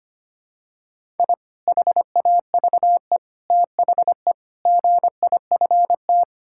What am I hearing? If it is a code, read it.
Morse code, heard as a keyed tone, about 25 wpm, 700 Hz. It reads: I HAVE THE GIFT
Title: Sentences from Top 1000 Words - Review Part 6 - 25wpm